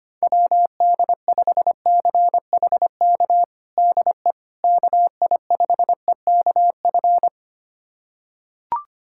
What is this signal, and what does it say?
Morse code, heard as a keyed tone, about 25 wpm, 700 Hz. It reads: WD5CHK DE KI5EKF
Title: USA Callsign DE Callsign 25wpm